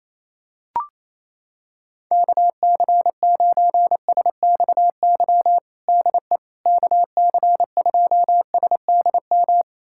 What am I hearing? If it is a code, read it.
Morse code, heard as a keyed tone, about 28 wpm, 700 Hz. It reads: KC9SXY DE KC2SDM
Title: USA Callsign DE Callsign - 6 Characters - 28wpm